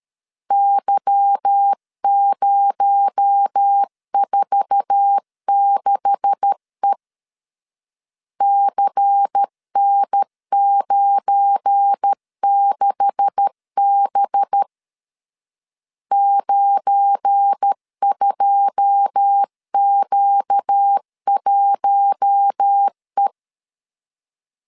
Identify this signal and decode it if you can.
Morse code, heard as a keyed tone, about 13 wpm, 800 Hz. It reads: Y046E CN96B 92Q1E
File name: LEZIONI / 9 286 12wpm 2305